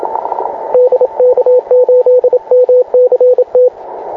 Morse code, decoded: DK8MCT